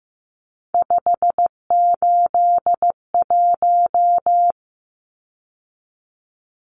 Morse code, heard as a keyed tone, about 15 wpm, 700 Hz. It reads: 581
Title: Numbers - 3-Digit 15wpm